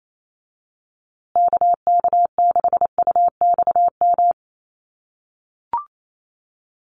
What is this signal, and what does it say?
Morse code, heard as a keyed tone, about 28 wpm, 700 Hz. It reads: KK6UXM
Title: USA Call Signs Repeated 2x 28wpm